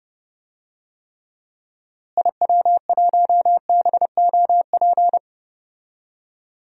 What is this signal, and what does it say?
Morse code, heard as a keyed tone, about 30 wpm, 700 Hz. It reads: IW1BOP
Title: DXCC-100 Call Signs - 6 Characters - Repeated 2x - 30wpm